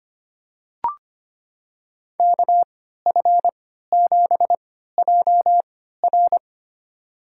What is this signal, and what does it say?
Morse code, heard as a keyed tone, about 25 wpm, spaced as for 14 wpm, 700 Hz. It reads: KF7JR
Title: USA Call Signs - 5 Characters - 3x Character Spacing - 25wpm